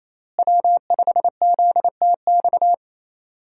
Morse code, heard as a keyed tone, about 28 wpm, 700 Hz. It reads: W5ZTX
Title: USA Call Signs - 5 Characters - Repeated 2x - 28wpm